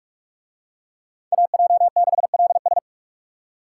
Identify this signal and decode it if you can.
Morse code, heard as a keyed tone, about 44 wpm, 700 Hz. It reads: AJ6LS